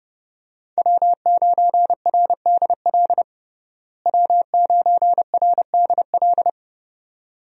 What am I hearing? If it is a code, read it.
Morse code, heard as a keyed tone, about 30 wpm, 700 Hz. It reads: W9RDL W9RDL
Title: USA Call Signs - 5 Characters - Repeated 2x - 30wpm